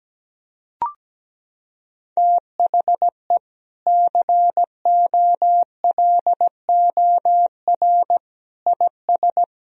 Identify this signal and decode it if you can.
Morse code, heard as a keyed tone, about 17 wpm, 700 Hz. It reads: THE COLOR IS
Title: Sentences from Top 1000 Words - Review Part 2 - 17wpm